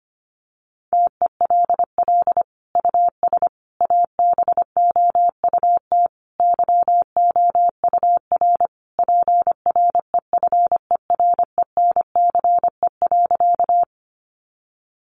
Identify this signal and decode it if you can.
Morse code, heard as a keyed tone, about 25 wpm, 700 Hz. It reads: TELL US ABOUT YOUR PREFERENCE.
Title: Sentences from Top 1500 Words 25wpm